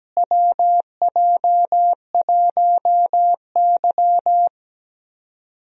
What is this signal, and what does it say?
Morse code, heard as a keyed tone, about 17 wpm, 700 Hz. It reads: WJ1Y
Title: USA Call Signs - 4 Characters - Repeated 3x - 17wpm